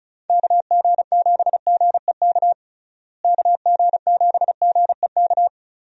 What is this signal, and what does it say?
Morse code, heard as a keyed tone, about 34 wpm, 700 Hz. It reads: KG7GEK KG7GEK
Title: USA Call Signs - 6 Characters - Repeated 2x - 35wpm